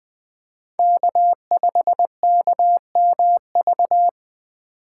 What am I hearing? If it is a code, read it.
Morse code, heard as a keyed tone, about 20 wpm, 700 Hz. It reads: K5KMV
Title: USA Call Signs - 5 Characters - 8x Character Spacing - 20wpm